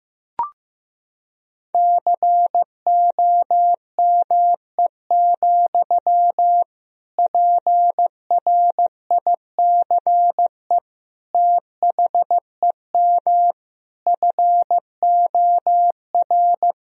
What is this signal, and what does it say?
Morse code, heard as a keyed tone, about 15 wpm, 700 Hz. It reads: COME, PRICE THEM FOR
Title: Sentences from Top 1000 Words - Review Part 4 - 15wpm